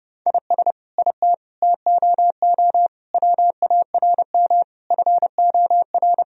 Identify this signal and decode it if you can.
Morse code, heard as a keyed tone, about 32 wpm, 700 Hz. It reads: IS IT TOO WARM FOR
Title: Sentences from Top 1000 Words - Review Part 2 - 30wpm